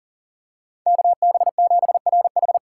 Morse code, heard as a keyed tone, about 40 wpm, 700 Hz. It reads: KB7RH
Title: USA Call Signs - 5 Characters - 4x Character Spacing - 40wpm